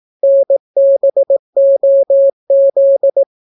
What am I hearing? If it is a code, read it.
Morse code, heard as a keyed tone, about 18 wpm, 550 Hz. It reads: NBOZ